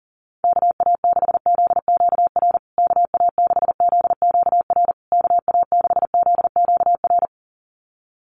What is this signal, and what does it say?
Morse code, heard as a keyed tone, about 40 wpm, 700 Hz. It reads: KA6ZQR KA6ZQR KA6ZQR
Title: USA Call Signs Repeated 3x 40wpm